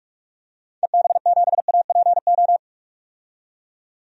Morse code, heard as a keyed tone, about 46 wpm, 700 Hz. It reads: EB7APO
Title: DXCC-100 Call Signs - 6 Characters - 45wpm